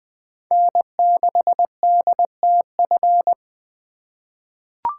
Morse code, heard as a keyed tone, about 20 wpm, 700 Hz. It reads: N6DTF
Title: USA Call Signs - 5 Characters - 4x Character Spacing - 20wpm